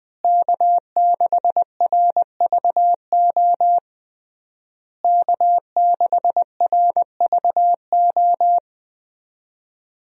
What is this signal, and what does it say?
Morse code, heard as a keyed tone, about 21 wpm, 700 Hz. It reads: K6RVO K6RVO
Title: USA Call Signs - 5 Characters - Repeated 3x - 20wpm